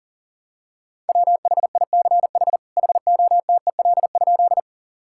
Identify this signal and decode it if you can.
Morse code, heard as a keyed tone, about 40 wpm, 700 Hz. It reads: WHICH HOTEL?
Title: Sentences from Top 1600 Words 40wpm